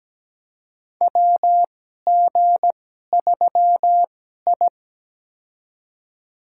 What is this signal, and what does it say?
Morse code, heard as a keyed tone, about 17 wpm, spaced as for 12 wpm, 700 Hz. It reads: WG3I
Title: USA Call Signs - 4 Characters - 2x Character Spacing - 17wpm